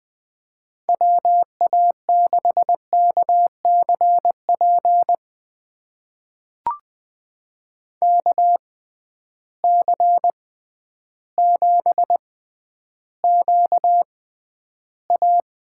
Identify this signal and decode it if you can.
Morse code, heard as a keyed tone, about 21 wpm, spaced as for 11 wpm, 700 Hz. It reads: WA6KCP KC7QA
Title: USA Call Signs - 6x Character Spacing - 20wpm